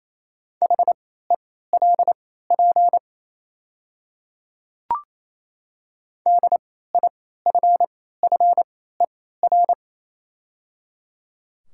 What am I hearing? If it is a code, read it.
Morse code, heard as a keyed tone, about 28 wpm, spaced as for 16 wpm, 700 Hz. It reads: HELP DIFFER
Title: 3x Character Spacing - Top 200 Words - 28wpm